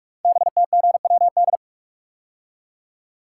Morse code, heard as a keyed tone, about 44 wpm, 700 Hz. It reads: BTGWD
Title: Code Groups - 5 Letters - 45wpm